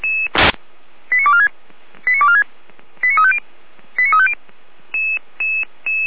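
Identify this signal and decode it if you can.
Five-tone selective call (ZVEI), heard as quick runs of five tones, tones of about 70 ms. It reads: 89246 89246 89350 89350
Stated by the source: ZVEI five-tone call received by SDR, logged by Feuerwehr Orlamünde (f85360).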